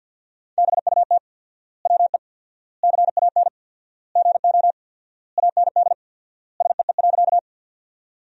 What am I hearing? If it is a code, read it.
Morse code, heard as a keyed tone, about 54 wpm, 700 Hz. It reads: BUT WE CAN GO AND SEE.
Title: 4x Word-Spacing - Sentences from Top 100 Words - 50wpm